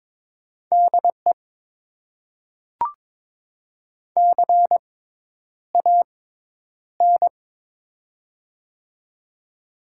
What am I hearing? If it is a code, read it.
Morse code, heard as a keyed tone, about 22 wpm, spaced as for 10 wpm, 700 Hz. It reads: DE CAN
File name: Top 90 QSO Elements - 6x Character Spacing - 22wpm